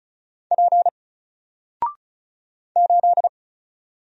Morse code, heard as a keyed tone, about 36 wpm, 700 Hz. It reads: P 8
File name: Single-Letter-Number-35wpm